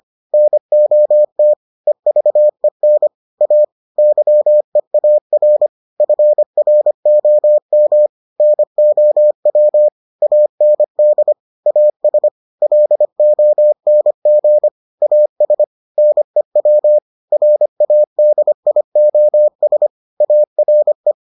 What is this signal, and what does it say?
Morse code, heard as a keyed tone, about 25 wpm, 600 Hz. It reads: NOT EVEN A YEAR FROM NOW AND AS LONG AS NEW RADIOS ARE